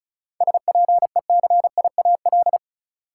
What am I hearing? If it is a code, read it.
Morse code, heard as a keyed tone, about 36 wpm, 700 Hz. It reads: SPECIAL
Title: Sentences from Top 1000 Words - Review Part 3 - New Word Drill - 35wpm